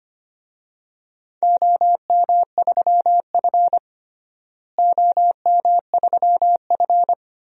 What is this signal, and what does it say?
Morse code, heard as a keyed tone, about 25 wpm, 700 Hz. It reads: OM3F OM3F
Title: DXCC-100 Call Signs - 4 Characters - Repeated 2x - 25wpm